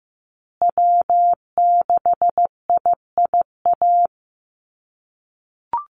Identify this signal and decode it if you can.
Morse code, heard as a keyed tone, about 15 wpm, 700 Hz. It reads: W6IIA
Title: USA Call Signs Repeated 2x 15wpm